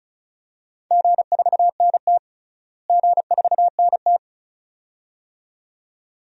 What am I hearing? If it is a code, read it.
Morse code, heard as a keyed tone, about 36 wpm, 700 Hz. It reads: G4NT G4NT
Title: DXCC-100 Call Signs - 4 Characters - Repeated 2x - 35wpm